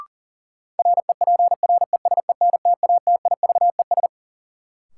Morse code, heard as a keyed tone, about 40 wpm, 700 Hz. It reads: REPRESENTATIVES
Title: Sets of 5 Words Encore - New Word Drill - 40wpm